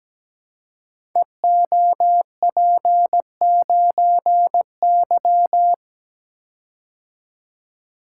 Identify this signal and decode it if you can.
Morse code, heard as a keyed tone, about 17 wpm, 700 Hz. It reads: EOP9Y
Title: Code Groups - 5 Letter:Numbers - 17wpm